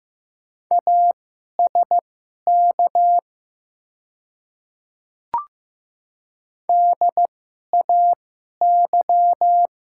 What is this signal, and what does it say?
Morse code, heard as a keyed tone, about 15 wpm, spaced as for 11 wpm, 700 Hz. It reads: ASK DAY
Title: 2x Character Spacing - Top 300 Words - 15wpm